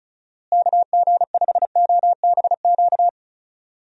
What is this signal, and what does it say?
Morse code, heard as a keyed tone, about 36 wpm, 700 Hz. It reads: KG5OBQ
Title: USA Call Signs - 6 Characters - Repeated 2x - 35wpm